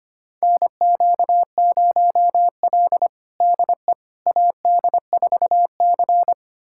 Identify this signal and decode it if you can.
Morse code, heard as a keyed tone, about 25 wpm, 700 Hz. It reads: NQ0L DE AD4C
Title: USA Callsign DE Callsign - 4 Characters - 25wpm